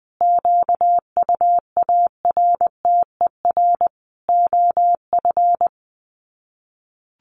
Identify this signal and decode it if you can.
Morse code, heard as a keyed tone, about 19 wpm, 700 Hz. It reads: QUARTER OF
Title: Sets of 5 Words Encore2 20wpm